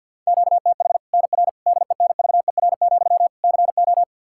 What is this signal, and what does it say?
Morse code, heard as a keyed tone, about 51 wpm, 700 Hz. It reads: QTH NR DENVER, CO
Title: K1USN SST QSO Practice 50wpm